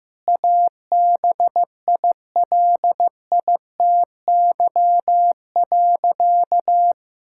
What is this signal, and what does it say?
Morse code, heard as a keyed tone, about 15 wpm, 700 Hz. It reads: ABILITY.
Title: Sentences from Top 1000 Words - Review Part 6 - 15wpm